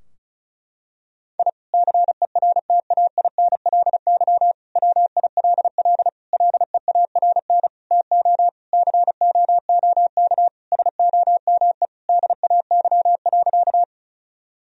Morse code, heard as a keyed tone, about 36 wpm, 700 Hz. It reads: I CERTAINLY WILL LEARN TO COOK SOME DAY.
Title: Sentences from Top 1000 Words - Review Part 4 - 35wpm